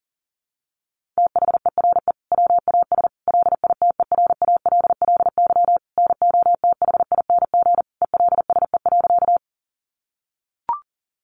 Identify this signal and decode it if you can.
Morse code, heard as a keyed tone, about 40 wpm, 700 Hz. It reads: THERE WAS LITERALLY NOTHING ELSE.
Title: Sentences from Top 1400 Words 40wpm